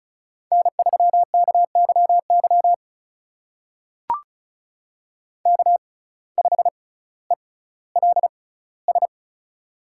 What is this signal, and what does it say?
Morse code, heard as a keyed tone, about 36 wpm, spaced as for 18 wpm, 700 Hz. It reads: N3KYY K5ELS
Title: USA Call Signs - 5 Characters - 6x Character Spacing - 35wpm